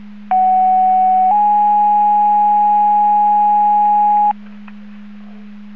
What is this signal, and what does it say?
Two-tone page, heard as a one-second tone, then a three-second tone, about 750 then 830 Hz. It is heard from 0.3 to 4.3 s.